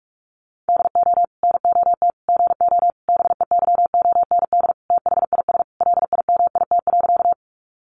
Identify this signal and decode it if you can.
Morse code, heard as a keyed tone, about 44 wpm, 700 Hz. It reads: DO NOT GO BEYOND THIS LIMIT.
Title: Sentences from Top 1500 Words 45wpm